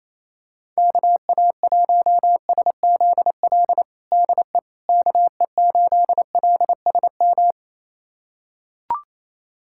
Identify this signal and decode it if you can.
Morse code, heard as a keyed tone, about 28 wpm, 700 Hz. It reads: KA1SZL DE KE8LSM
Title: USA Callsign DE Callsign - 6 Characters - 28wpm